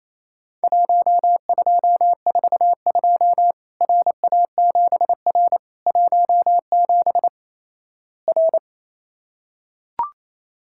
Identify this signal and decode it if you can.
Morse code, heard as a keyed tone, about 28 wpm, 700 Hz. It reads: 1242 RA7R 17 R
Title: Worked All Europe DX Contest - QTC Exchange - 28wpm